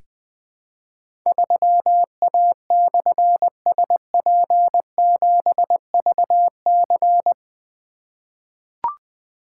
Morse code, heard as a keyed tone, about 20 wpm, 700 Hz. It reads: 3A/SP7VC
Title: Difficult Call Signs 20wpm